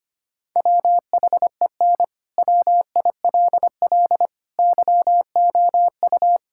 Morse code, heard as a keyed tone, about 25 wpm, 700 Hz. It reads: WHEN WILL YOU
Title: Sentences from Top 1000 Words - Review Part 6 - 25wpm